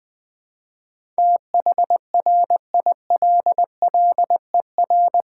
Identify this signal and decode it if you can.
Morse code, heard as a keyed tone, about 20 wpm, 690 Hz. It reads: THRILLER
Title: Difficult Words with Hints - 20wpm